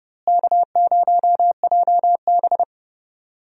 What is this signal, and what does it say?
Morse code, heard as a keyed tone, about 31 wpm, 700 Hz. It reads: K0JB
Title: USA Call Signs - 4 Characters - Repeated 3x - 30wpm